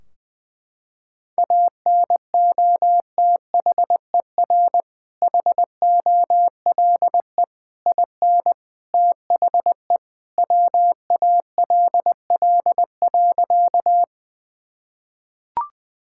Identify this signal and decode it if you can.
Morse code, heard as a keyed tone, about 20 wpm, 700 Hz. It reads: ANOTHER HOLE IN THE WALL.
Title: Sentences from Top 1600 Words 20wpm